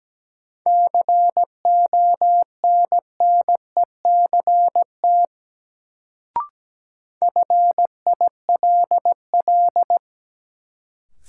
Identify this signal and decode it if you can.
Morse code, heard as a keyed tone, about 17 wpm, 700 Hz. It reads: CONNECT FILL
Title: Sentences from Top 1500 Words - New Word Drill Alt - 17wpm